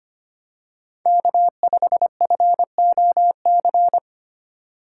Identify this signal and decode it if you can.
Morse code, heard as a keyed tone, about 25 wpm, 700 Hz. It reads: K5FOC
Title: USA Call Signs - 5 Characters - 3x Character Spacing - 25wpm